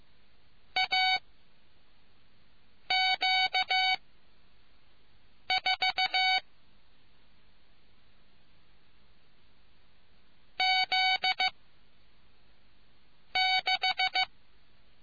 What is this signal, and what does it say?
Morse code, heard as a keyed tone, about 15 wpm, 750 Hz. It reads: A Q 4 Z 6